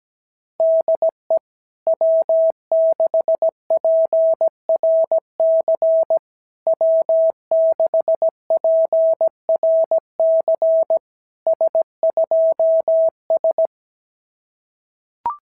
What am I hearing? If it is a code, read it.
Morse code, heard as a keyed tone, about 17 wpm, 650 Hz. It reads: DE W6PRC W6PRC S2S
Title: SOTA QSO Practice 17wpm